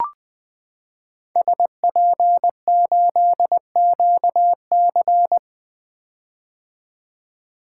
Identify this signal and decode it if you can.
Morse code, heard as a keyed tone, about 20 wpm, 700 Hz. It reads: SP8QC
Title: 2023 POTA Hunter Call Signs - 20wpm